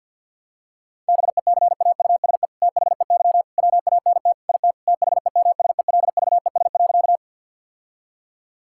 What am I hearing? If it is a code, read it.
Morse code, heard as a keyed tone, about 50 wpm, 700 Hz. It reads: BECAUSE THEY WANT IT THEMSELVES.
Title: Sentences from Top 1000 Words - Review Part 5 - 50wpm